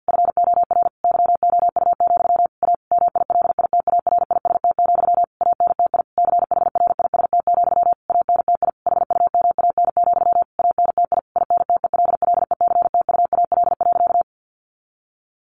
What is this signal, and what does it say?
Morse code, heard as a keyed tone, about 50 wpm, 700 Hz. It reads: FOR YOU, A MILITARIST, ANTI CHRIST, ANTI HUMAN, ANTI INTELLECTUAL.
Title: A Sound Of Thunder 50wpm